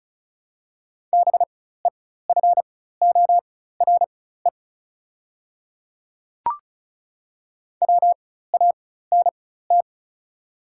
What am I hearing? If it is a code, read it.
Morse code, heard as a keyed tone, about 36 wpm, spaced as for 16 wpm, 700 Hz. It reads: BEFORE WANT
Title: 4x Character Spacing - Top 300 Words - 35wpm